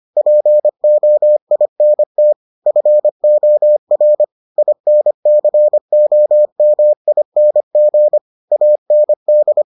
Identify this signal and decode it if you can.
Morse code, heard as a keyed tone, about 25 wpm, 600 Hz. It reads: POINT FOR INCOMING AND